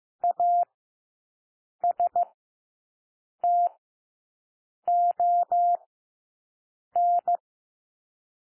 Morse code, heard as a keyed tone, about 15 wpm, 700 Hz. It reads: A S T O N
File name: cwmp3 / W1 SOUND WORDS 15 wpm